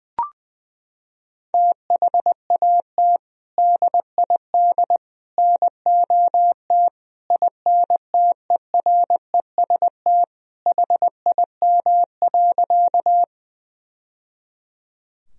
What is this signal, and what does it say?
Morse code, heard as a keyed tone, about 20 wpm, 700 Hz. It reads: THAT DID NOT INTEREST HIM.
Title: Sentences from Top 1000 Words - Review Part 2 - 20wpm